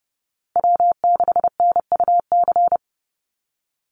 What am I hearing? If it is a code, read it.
Morse code, heard as a keyed tone, about 30 wpm, 700 Hz. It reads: W6NUC
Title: Call Signs Repeated 2x - 30wpm